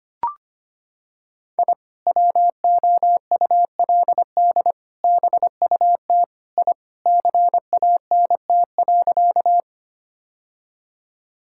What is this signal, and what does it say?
Morse code, heard as a keyed tone, about 25 wpm, 700 Hz. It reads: I WOULD BUT I CANT.
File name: practice / 25wpm / Sentences with 2-Word Phrases - 25wpm